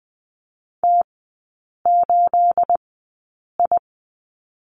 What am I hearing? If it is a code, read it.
Morse code, heard as a keyed tone, about 20 wpm, spaced as for 8 wpm, 690 Hz. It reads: T8I
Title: 212 Lesson 34e - Number 8 - Three Characters 20wpm